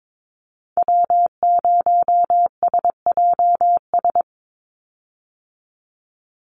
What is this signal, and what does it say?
Morse code, heard as a keyed tone, about 22 wpm, 700 Hz. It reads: W0SJS
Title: Call Signs Repeated 3x - 22wpm